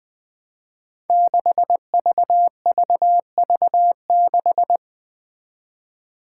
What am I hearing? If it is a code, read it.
Morse code, heard as a keyed tone, about 20 wpm, 700 Hz. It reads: 6VVV6
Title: Five Challenging Letter-Number Combinations 20wpm